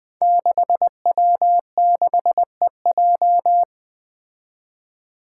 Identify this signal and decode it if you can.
Morse code, heard as a keyed tone, about 20 wpm, 700 Hz. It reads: 6W6EJ
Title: Five Challenging Letter-Number Combinations 20wpm